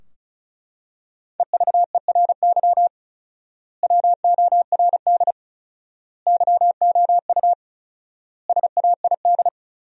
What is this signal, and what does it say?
Morse code, heard as a keyed tone, about 34 wpm, 700 Hz. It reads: EVERY WORD YOU SAID
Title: 4x Word-Spacing - Sentences from Top 400 Words - 35wpm